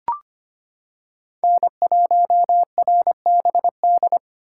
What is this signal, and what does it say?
Morse code, heard as a keyed tone, about 25 wpm, 700 Hz. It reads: N1RBD
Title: K1USN SST QSO Practice - 25wpm